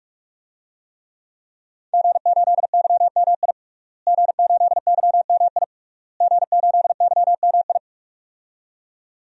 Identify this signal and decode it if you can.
Morse code, heard as a keyed tone, about 46 wpm, 700 Hz. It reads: G8YMI G8YMI G8YMI
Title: DXCC-100 Call Signs - 5 Characters - Repeated 3x - 45wpm